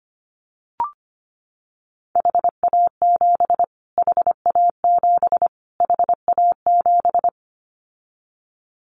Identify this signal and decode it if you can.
Morse code, heard as a keyed tone, about 25 wpm, 700 Hz. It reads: HA7 HA7 HA7
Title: Callsign Prefixes for Top 25 DXCC - Repeated 3x - 25wpm